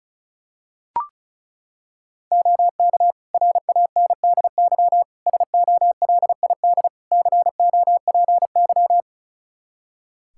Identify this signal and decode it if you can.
Morse code, heard as a keyed tone, about 34 wpm, 700 Hz. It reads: OK RANDY SOLID COPY
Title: Practice QSO - Exchange 3 - More Info - 35wpm